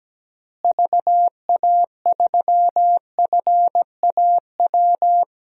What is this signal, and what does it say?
Morse code, heard as a keyed tone, about 17 wpm, 700 Hz. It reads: VA3FAW
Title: DXCC-100 Call Signs - 6 Characters - Repeated 3x - 17wpm